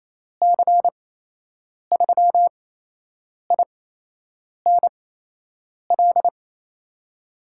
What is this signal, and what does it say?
Morse code, heard as a keyed tone, about 28 wpm, spaced as for 8 wpm, 700 Hz. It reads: C3INL
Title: USA Call Signs - 6 Characters - 8x Character Spacing - 28wpm